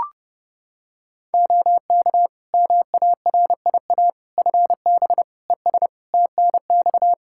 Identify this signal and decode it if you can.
Morse code, heard as a keyed tone, about 30 wpm, 700 Hz. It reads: OK MARIA FB ES TNX
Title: Practice QSO - Exchange 2 - Info - 30wpm